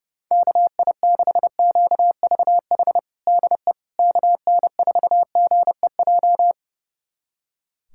Morse code, heard as a keyed tone, about 32 wpm, 700 Hz. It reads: KI6QVH DE KN4GEJ
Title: USA Callsign DE Callsign 30wpm